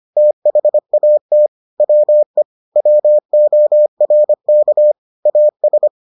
Morse code, heard as a keyed tone, about 25 wpm, 600 Hz. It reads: THAT WE WORK AS